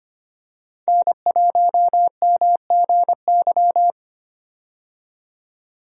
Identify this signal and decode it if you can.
Morse code, heard as a keyed tone, about 25 wpm, 700 Hz. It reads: N1MGY
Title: USA Call Signs - 5 Characters - 25wpm